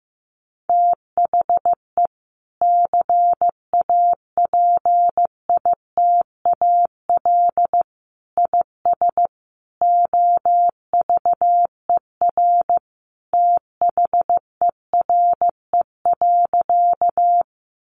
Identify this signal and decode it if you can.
Morse code, heard as a keyed tone, about 15 wpm, 700 Hz. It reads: THE CAPITAL IS OVER THERE.
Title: Sentences from Top 1000 Words 15wpm